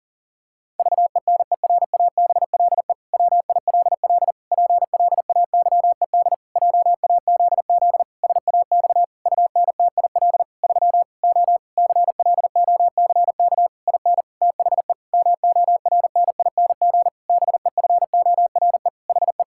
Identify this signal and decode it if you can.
Morse code, heard as a keyed tone, about 40 wpm, 700 Hz. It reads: VENERABLE WILL PLAYED JAZZ SAX UNTIL 3 O CLOCK IN THE MORNING BEFORE HE
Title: Pangram Challenge v2 - 40wpm